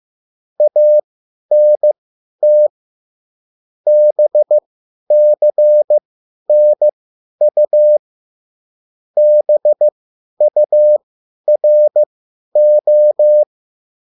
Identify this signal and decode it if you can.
Morse code, heard as a keyed tone, about 15 wpm, spaced as for 10 wpm, 600 Hz. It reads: ANT BCNU BURO
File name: QSL / CW lesson